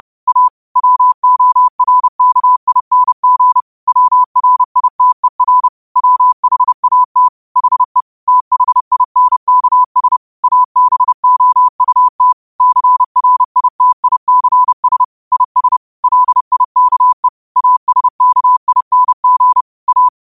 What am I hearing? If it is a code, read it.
Morse code, heard as a keyed tone, about 30 wpm, 1000 Hz. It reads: A WORKING WRITER WHAT HE THINKS ABOUT CRITICS IS LIKE ASKING A
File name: qotdfiles / 30wpm